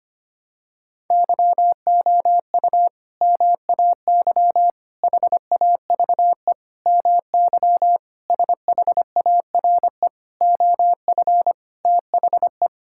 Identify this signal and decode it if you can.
Morse code, heard as a keyed tone, about 25 wpm, 700 Hz. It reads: YOU MAY HAVE MY SHARE OF THE
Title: Sentences from Top 1000 Words - Review Part 6 - 25wpm